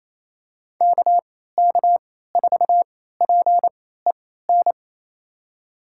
Morse code, heard as a keyed tone, about 28 wpm, spaced as for 16 wpm, 700 Hz. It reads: KK4PEN
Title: USA Call Signs - 6 Characters - 3x Character Spacing - 28wpm